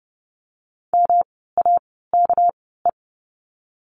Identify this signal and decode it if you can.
Morse code, heard as a keyed tone, about 30 wpm, spaced as for 17 wpm, 700 Hz. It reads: MAKE
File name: course / 30:10wpm / 230 Lesson 37f - Letter x - Words 30wpm